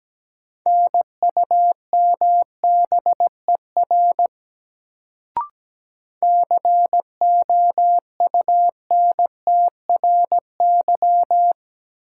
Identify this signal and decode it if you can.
Morse code, heard as a keyed tone, about 17 wpm, 700 Hz. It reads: NUMBER COUNTRY
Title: Sets of 2 Words - New Word Drill - 17wpm